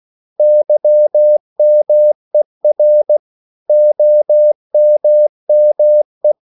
Morse code, heard as a keyed tone, about 16 wpm, 600 Hz. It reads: YMER OMME
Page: Korte danske ord 16wpm